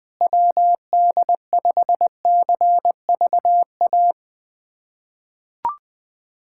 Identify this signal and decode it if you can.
Morse code, heard as a keyed tone, about 20 wpm, 700 Hz. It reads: WD5CVA
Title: USA Callsign DE Callsign - 6 Characters - 20wpm